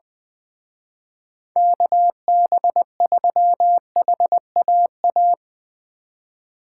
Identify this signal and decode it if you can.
Morse code, heard as a keyed tone, about 21 wpm, 700 Hz. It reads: KB3HAA